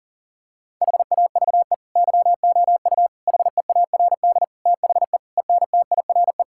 Morse code, heard as a keyed tone, about 40 wpm, 700 Hz. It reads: HAVE YOU HEARD THE ENTIRE
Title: Sentences from Top 1000 Words - Review Part 5 - 40wpm